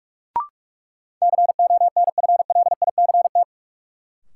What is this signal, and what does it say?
Morse code, heard as a keyed tone, about 47 wpm, 700 Hz. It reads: CONFLICT
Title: Sentences from Top 1300 Words - New Word Drill - 45wpm